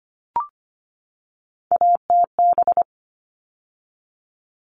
Morse code, heard as a keyed tone, about 25 wpm, 700 Hz. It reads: ATB
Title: All Cut Numbers - 3-Digit 25wpm